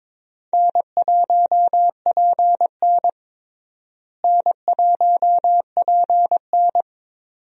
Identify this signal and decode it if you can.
Morse code, heard as a keyed tone, about 22 wpm, 700 Hz. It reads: N1PN N1PN
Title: USA Call Signs - 4 Characters - Repeated 2x - 22wpm